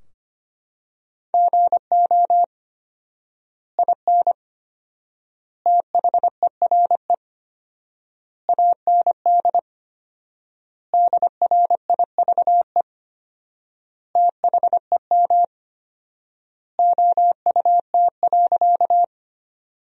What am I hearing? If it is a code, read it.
Morse code, heard as a keyed tone, about 25 wpm, 700 Hz. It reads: GO IN THERE AND DRIVE THEM OUT.
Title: 4x Word-Spacing - Sentences from Top 500 Words - 25wpm